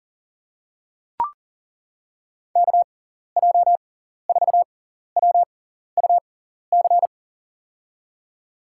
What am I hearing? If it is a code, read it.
Morse code, heard as a keyed tone, about 40 wpm, spaced as for 14 wpm, 700 Hz. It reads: KJ4WUC
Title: USA Call Signs - 6x Character Spacing - 40wpm